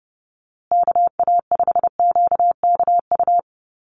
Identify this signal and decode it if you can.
Morse code, heard as a keyed tone, about 30 wpm, 700 Hz. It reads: KA5QKU
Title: USA Call Signs Repeated 2x 30wpm